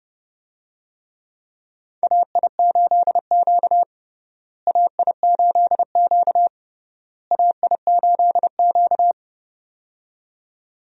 Morse code, heard as a keyed tone, about 30 wpm, 700 Hz. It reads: AI8Q AI8Q AI8Q
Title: USA Call Signs - 4 Characters - Repeated 3x - 30wpm